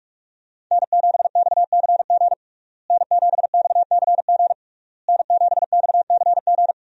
Morse code, heard as a keyed tone, about 42 wpm, 700 Hz. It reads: N7XCG N7XCG N7XCG
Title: USA Call Signs - 5 Characters - Repeated 3x - 45wpm